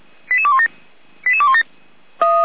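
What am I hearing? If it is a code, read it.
Five-tone selective call (ZVEI), heard as quick runs of five tones, tones of about 80 ms. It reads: 88217 88217
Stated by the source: ZVEI five-tone call received by SDR, from Feuerwehr Orlamünde audio server.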